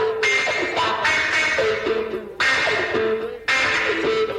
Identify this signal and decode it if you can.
Emergency Alert System (SAME) tones, heard as harsh buzzing bursts. It is heard from 1.1 to 3.8 s.
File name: wrct-log hi- 09